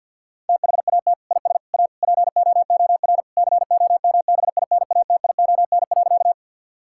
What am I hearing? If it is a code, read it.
Morse code, heard as a keyed tone, about 48 wpm, 700 Hz. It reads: THAT IS A POOR COMBINATION.
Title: Sentences from Top 1000 Words - Review Part 6 - 50wpm